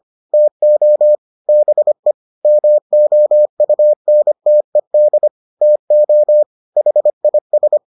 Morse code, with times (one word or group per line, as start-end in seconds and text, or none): 0.3-1.2 TO
1.5-2.1 BE
2.4-5.3 MOUNTED
5.6-6.4 TO
6.8-7.8 HIS